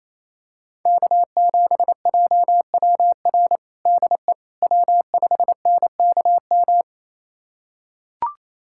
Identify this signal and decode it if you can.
Morse code, heard as a keyed tone, about 28 wpm, 700 Hz. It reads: K7JWR DE W5NKM
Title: USA Callsign DE Callsign 28wpm